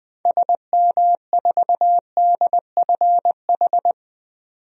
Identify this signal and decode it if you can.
Morse code, heard as a keyed tone, about 20 wpm, 700 Hz. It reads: SM4DFH
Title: DXCC-100 Call Signs - 6 Characters - 20wpm